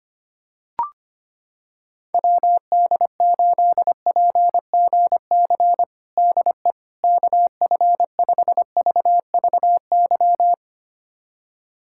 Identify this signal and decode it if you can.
Morse code, heard as a keyed tone, about 25 wpm, 700 Hz. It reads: WD8PGC DE KF5VVY
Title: USA Callsign DE Callsign - 6 Characters - 25wpm